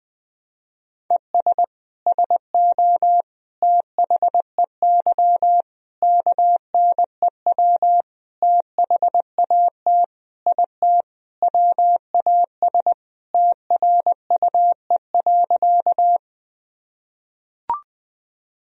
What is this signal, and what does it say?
Morse code, heard as a keyed tone, about 20 wpm, 700 Hz. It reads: ES SO THEY KNEW THAT IT WAS TRUE.
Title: Sentences from Top 400 Words and Abbreviations 20wpm